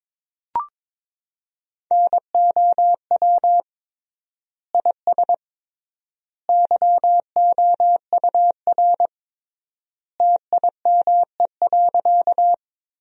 Morse code, heard as a keyed tone, about 22 wpm, 700 Hz. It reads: NOW IS YOUR TIME.
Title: 3x Word-Spacing - Sentences from Top 100 Words - 22wpm